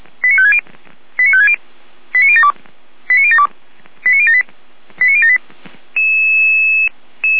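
Five-tone selective call (ZVEI), heard as quick runs of five tones, tones of about 70 ms. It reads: 89560 89560 89072 89072 89078 89078
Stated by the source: ZVEI five-tone call 89170 received by SDR.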